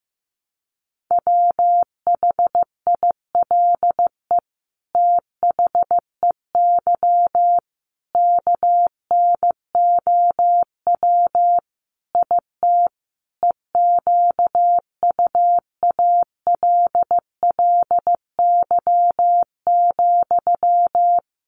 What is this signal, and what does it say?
Morse code, heard as a keyed tone, about 15 wpm, 700 Hz. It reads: WHILE THEY KNOW IT EQUALLY,
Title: Sentences from Top 1400 Words 15wpm